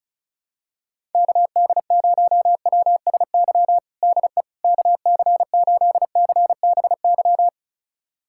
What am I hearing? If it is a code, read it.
Morse code, heard as a keyed tone, about 34 wpm, 690 Hz. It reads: KD0WSY DE KC8CBY
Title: USA Callsign DE Callsign 35wpm